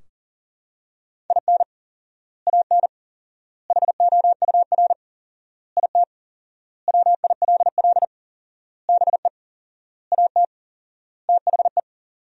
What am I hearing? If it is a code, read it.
Morse code, heard as a keyed tone, about 37 wpm, 700 Hz. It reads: IN AN HOUR IT WILL BE AT THE
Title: 4x Word-Spacing - Sentences from Top 400 Words - 40wpm